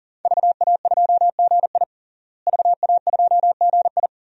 Morse code, heard as a keyed tone, about 40 wpm, 700 Hz. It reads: VA2GI VA2GI
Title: DXCC-100 Call Signs - 5 Characters - Repeated 2x - 40wpm